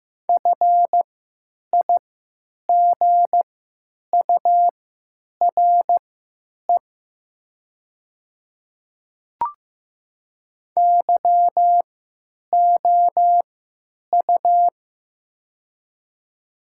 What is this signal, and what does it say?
Morse code, heard as a keyed tone, about 15 wpm, spaced as for 9 wpm, 700 Hz. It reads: FIGURE YOU
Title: 3x Character Spacing - Top 500 Words - 15wpm